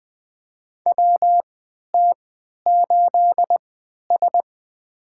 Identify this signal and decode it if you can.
Morse code, heard as a keyed tone, about 21 wpm, spaced as for 11 wpm, 700 Hz. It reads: WT8S
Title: USA Call Signs - 4 Characters - 3x Character Spacing - 20wpm